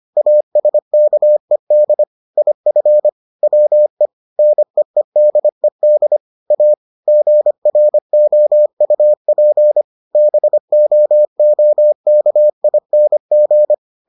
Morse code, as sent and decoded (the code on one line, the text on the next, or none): .- ... -.- . -.. / .. ..-. / .-- . / -. . -.. -.. / .- / --. .-. --- ..- .--. / -... --- --- -.- .. -. --.
ASKED IF WE NEEDED A GROUP BOOKING